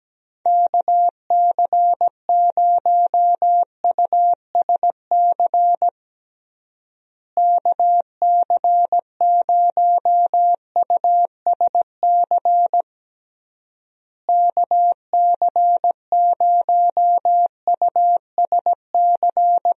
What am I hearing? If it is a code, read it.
Morse code, heard as a keyed tone, about 17 wpm, 700 Hz. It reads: KC0USC KC0USC KC0USC
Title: USA Call Signs - 6 Characters - Repeated 3x - 17wpm